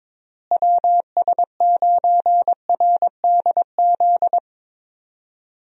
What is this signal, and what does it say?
Morse code, heard as a keyed tone, about 22 wpm, 700 Hz. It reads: WS9RDZ
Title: USA Call Signs - 6 Characters - Repeated 2x - 22wpm